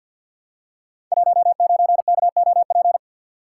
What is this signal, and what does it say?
Morse code, heard as a keyed tone, about 52 wpm, 700 Hz. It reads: 19KOP
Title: Code Groups - 5 Letter:Numbers - 50wpm